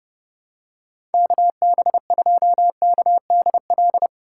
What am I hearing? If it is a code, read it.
Morse code, heard as a keyed tone, about 30 wpm, 700 Hz. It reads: KB2KDL
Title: USA Call Signs - 6 Characters - 4x Character Spacing - 30wpm